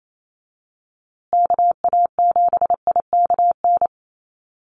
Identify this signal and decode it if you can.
Morse code, heard as a keyed tone, about 28 wpm, 700 Hz. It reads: KA7IKN
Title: USA Call Signs Repeated 2x 28wpm